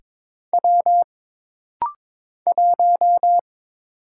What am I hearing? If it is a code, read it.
Morse code, heard as a keyed tone, about 22 wpm, 700 Hz. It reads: W 1
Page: This rapid-fire timing is geared to push your ICR (Instant Character Recognition) skills.